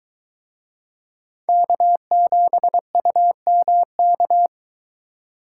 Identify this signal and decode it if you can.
Morse code, heard as a keyed tone, about 22 wpm, 700 Hz. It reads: K7UMK